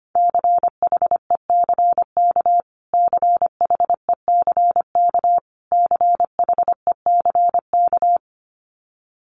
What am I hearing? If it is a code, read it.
Morse code, heard as a keyed tone, about 25 wpm, 700 Hz. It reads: CHECK CHECK CHECK
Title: Top 500 Words Repeated 25wpm